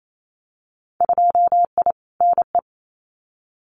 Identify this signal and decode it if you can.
Morse code, heard as a keyed tone, about 28 wpm, 690 Hz. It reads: 2I NE
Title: Winter Field Day Exchanges Encore 28wpm